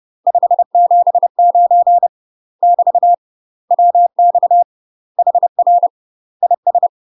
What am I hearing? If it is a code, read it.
Morse code, heard as a keyed tone, about 30 wpm, 700 Hz. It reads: 579 <BT> WX HR IS
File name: UK MASTER QSOs TRAINING / 30WPM / UK MASTER QSOs 0040